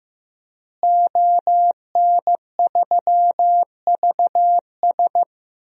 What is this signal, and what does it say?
Morse code, heard as a keyed tone, about 15 wpm, 700 Hz. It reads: ON3VS
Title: DXCC-100 Call Signs - 5 Characters - Repeated 3x - 15wpm